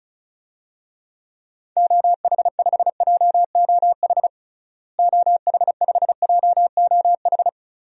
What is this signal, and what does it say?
Morse code, heard as a keyed tone, about 34 wpm, 700 Hz. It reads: OH5JOH OH5JOH
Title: DXCC-100 Call Signs - 6 Characters - Repeated 2x - 35wpm